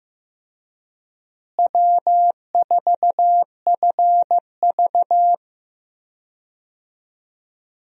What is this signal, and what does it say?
Morse code, heard as a keyed tone, about 15 wpm, 700 Hz. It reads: W4FV
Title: USA Call Signs - 4 Characters - Repeated 2x - 15wpm